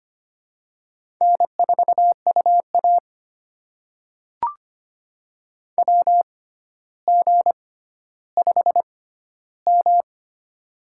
Morse code, heard as a keyed tone, about 25 wpm, spaced as for 13 wpm, 700 Hz. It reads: N4UA WG5M
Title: USA Call Signs - 4 Characters - 6x Character Spacing - 25wpm